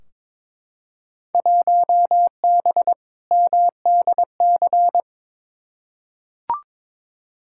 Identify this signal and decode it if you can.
Morse code, heard as a keyed tone, about 22 wpm, 700 Hz. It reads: 1B MDC
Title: Field Day Exchanges - 22wpm